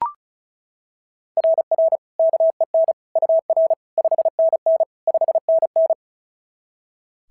Morse code, heard as a keyed tone, about 33 wpm, 660 Hz. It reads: RR KEN UR 5NN 5NN
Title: SOTA QSO Practice 35wpm